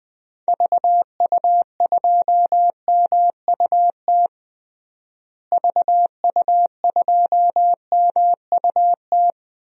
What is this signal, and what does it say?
Morse code, heard as a keyed tone, about 20 wpm, 700 Hz. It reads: VU2MUT VU2MUT
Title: DXCC-100 Call Signs - 6 Characters - Repeated 2x - 20wpm